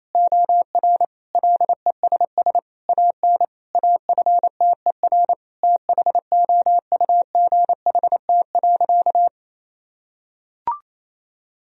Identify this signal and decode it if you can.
Morse code, heard as a keyed tone, about 28 wpm, 700 Hz. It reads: OR LESS AN AFTER THOUGHT.
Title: Sentences from Top 1000 Words - Review Part 2 - 28wpm